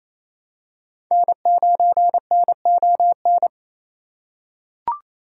Morse code, heard as a keyed tone, about 28 wpm, 700 Hz. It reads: N9NON
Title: USA Call Signs - 5 Characters - 2x Character Spacing - 28wpm